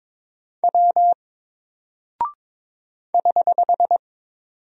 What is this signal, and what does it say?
Morse code, heard as a keyed tone, about 22 wpm, 700 Hz. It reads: W <HH>
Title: Single Character (Letter:Number:Punctuation:Prosign) Alt 22wpm